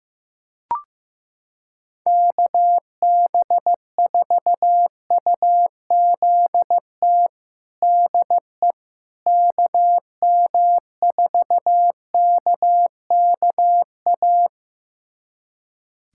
Morse code, heard as a keyed tone, about 15 wpm, 700 Hz. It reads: KB4UZT DE KM4KKA
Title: USA Callsign DE Callsign - 6 Characters - 15wpm